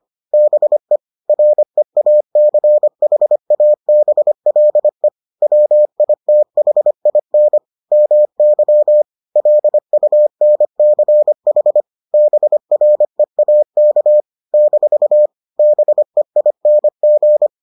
Morse code, heard as a keyed tone, about 25 wpm, 600 Hz. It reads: BE REACHABLE WITHIN MY LUNCH BREAK - BEING